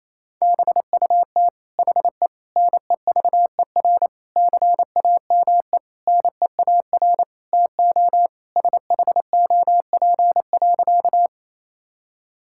Morse code, heard as a keyed tone, about 28 wpm, 700 Hz. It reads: BUT HE NEVER CAME NEAR TO SHOP.
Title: Sentences from Top 1600 Words 28wpm